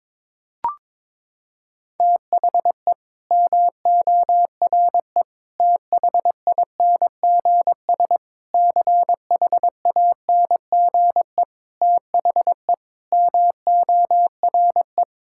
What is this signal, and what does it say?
Morse code, heard as a keyed tone, about 21 wpm, 700 Hz. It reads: THE MORE THINGS CHANGE THE MORE
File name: practice / 22wpm / Sets of 8 Words - 22wpm